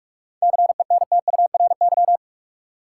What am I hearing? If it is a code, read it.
Morse code, heard as a keyed tone, about 46 wpm, 700 Hz. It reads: CENTURY
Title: Sets of 5 Words - New Word Drill - 45wpm